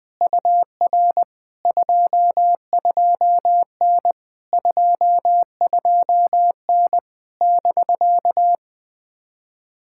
Morse code, heard as a keyed tone, about 20 wpm, 700 Hz. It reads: UR 22N 22N <BK>
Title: POTA QSO Practice - 20wpm